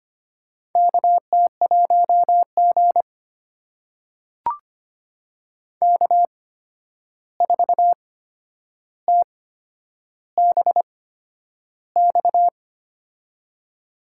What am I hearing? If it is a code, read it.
Morse code, heard as a keyed tone, about 25 wpm, spaced as for 10 wpm, 700 Hz. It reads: KT1G K4TBX
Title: USA Call Signs - 8x Character Spacing - 25wpm